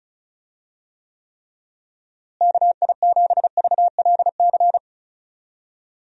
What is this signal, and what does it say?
Morse code, heard as a keyed tone, about 36 wpm, 700 Hz. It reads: KI7VLC
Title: USA Call Signs - 6 Characters - Repeated 3x - 35wpm